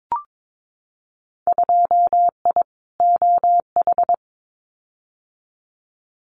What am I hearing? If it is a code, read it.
Morse code, heard as a keyed tone, about 22 wpm, 700 Hz. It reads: 2I OH
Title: Winter Field Day Exchanges 22wpm